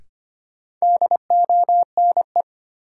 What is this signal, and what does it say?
Morse code, heard as a keyed tone, about 25 wpm, 700 Hz. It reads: DONE
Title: Sets of 5 Words Encore - New Word Drill - 25wpm